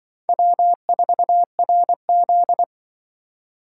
Morse code, heard as a keyed tone, about 25 wpm, 700 Hz. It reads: W4RZ
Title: Speed Racing - USA Call Signs 20wpm